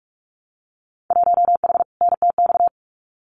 Morse code, heard as a keyed tone, about 46 wpm, 700 Hz. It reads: 1H NTX
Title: Winter Field Day Exchanges 45wpm